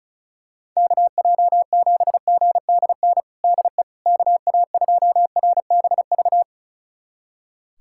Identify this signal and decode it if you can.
Morse code, heard as a keyed tone, about 34 wpm, 700 Hz. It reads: KJ7GDN DE KA2RBV
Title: USA Callsign DE Callsign - 6 Characters - 35wpm